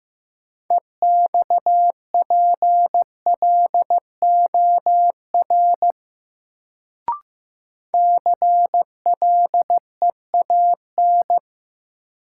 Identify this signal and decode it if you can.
Morse code, heard as a keyed tone, about 15 wpm, 700 Hz. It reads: EXPLOR CLEAN
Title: Sentences from Top 1100 Words - New Word Drill - 15wpm